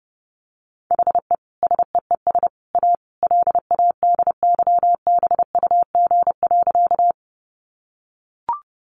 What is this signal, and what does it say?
Morse code, heard as a keyed tone, about 30 wpm, 700 Hz. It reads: HE SEES A LADYBUG.
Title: Additional Easy Phrases 30wpm